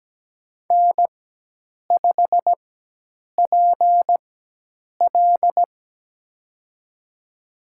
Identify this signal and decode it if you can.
Morse code, heard as a keyed tone, about 17 wpm, spaced as for 8 wpm, 700 Hz. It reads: N5PL
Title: USA Call Signs - 4 Characters - 4x Character Spacing - 17wpm